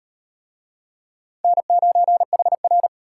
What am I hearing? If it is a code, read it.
Morse code, heard as a keyed tone, about 33 wpm, 700 Hz. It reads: N9HR